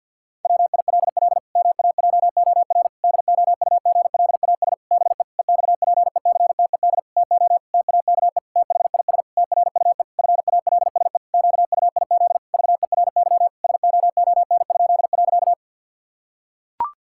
Morse code, heard as a keyed tone, about 50 wpm, 700 Hz. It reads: WILL MAJOR DOUGLAS BE EXPECTED TO TAKE THIS TRUE FALSE QUIZ VERY SOON?.
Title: Pangram Challenge v2 - 50wpm